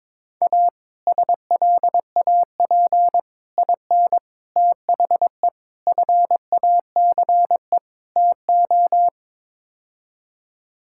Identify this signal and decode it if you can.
Morse code, heard as a keyed tone, about 21 wpm, 700 Hz. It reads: A SLAP IN THE FACE TO
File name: practice / 22wpm / Sets of 6 Words - 22wpm